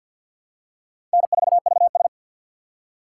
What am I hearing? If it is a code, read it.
Morse code, heard as a keyed tone, about 46 wpm, 700 Hz. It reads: N4VS
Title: USA Call Signs - 4 Characters - 3x Character Spacing - 50wpm